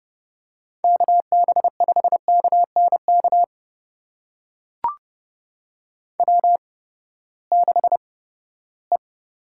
Morse code, heard as a keyed tone, about 30 wpm, spaced as for 17 wpm, 700 Hz. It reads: KB5KNK W6E
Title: USA Call Signs - 8x Character Spacing - 30wpm